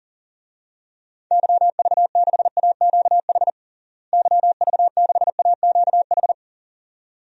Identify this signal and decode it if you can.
Morse code, heard as a keyed tone, about 37 wpm, 700 Hz. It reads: YV6AQH YV6AQH
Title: DXCC-100 Call Signs - 6 Characters - Repeated 2x - 40wpm